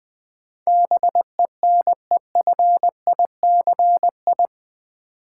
Morse code, heard as a keyed tone, about 20 wpm, 700 Hz. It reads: BENEFICI
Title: Build A Word - 4 Letters At A Time - 11-Letter Words - 20wpm